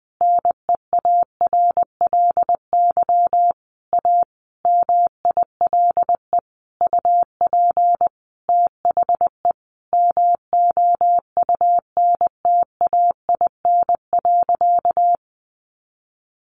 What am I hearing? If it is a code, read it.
Morse code, heard as a keyed tone, about 20 wpm, 700 Hz. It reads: NEARLY A MILE UP THE MOUNTAIN.
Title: Sentences from Top 1300 Words 20wpm